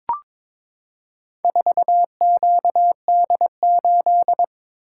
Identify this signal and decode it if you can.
Morse code, heard as a keyed tone, about 22 wpm, 700 Hz. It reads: 4QD8
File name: practice / 22wpm / Four Letter-Number Combinations - 22wpm